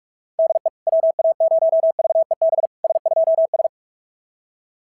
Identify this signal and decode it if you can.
Morse code, heard as a keyed tone, about 46 wpm, 660 Hz. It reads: DE WA0VEB S2S
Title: SOTA QSO Practice 45wpm